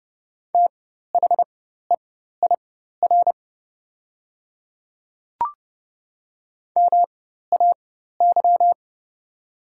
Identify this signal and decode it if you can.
Morse code, heard as a keyed tone, about 30 wpm, spaced as for 14 wpm, 700 Hz. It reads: THEIR MAY
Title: 4x Character Spacing - Top 100 Words - 30wpm